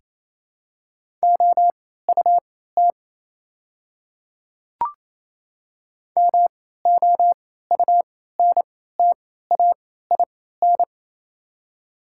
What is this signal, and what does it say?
Morse code, heard as a keyed tone, about 27 wpm, spaced as for 16 wpm, 700 Hz. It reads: OUT MOUNTAIN
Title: 3x Character Spacing - Top 500 Words - 28wpm